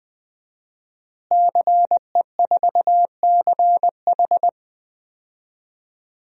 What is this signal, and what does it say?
Morse code, heard as a keyed tone, about 20 wpm, 700 Hz. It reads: CE4CH
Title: DXCC-100 Call Signs - 5 Characters - Repeated 2x - 20wpm